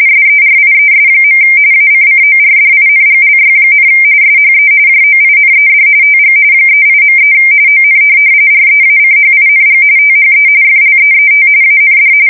RTTY, about 45.45 baud. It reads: SERIES. ACCORDINGLY, SHE JOINED TG 38.3 ON REACHING THE FLEET AND
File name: fsk170 / USS-Massachusetts